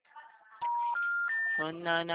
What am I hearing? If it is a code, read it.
Special information tone, with speech.
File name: out- -2004- - - .74041